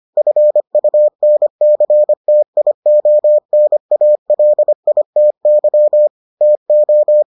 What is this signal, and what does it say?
Morse code, heard as a keyed tone, about 25 wpm, 600 Hz. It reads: FUNCTIONALITY TO